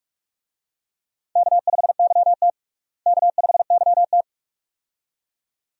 Morse code, heard as a keyed tone, about 42 wpm, 700 Hz. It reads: K5YT K5YT
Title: USA Call Signs - 4 Characters - Repeated 2x - 45wpm